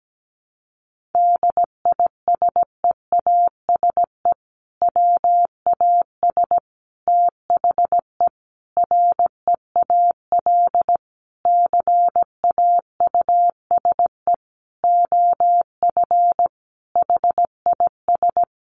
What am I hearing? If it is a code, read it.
Morse code, heard as a keyed tone, about 17 wpm, 700 Hz. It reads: DISEASE WAS THE REAL CAUSE OF HIS
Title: Sentences from Top 1100 Words 17wpm